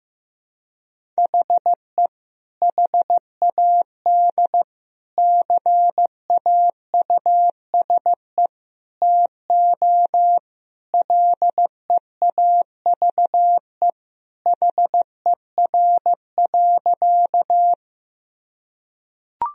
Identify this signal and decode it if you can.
Morse code, heard as a keyed tone, about 15 wpm, 700 Hz. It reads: HE HAD CAUSE TO LEAVE HER.
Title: Sentences from Top 1000 Words - Review Part 1 - 15wpm